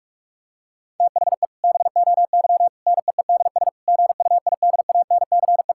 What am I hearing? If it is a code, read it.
Morse code, heard as a keyed tone, about 42 wpm, 700 Hz. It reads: THE BOY NEEDS GUIDANCE
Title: Sentences from Top 1600 Words 45wpm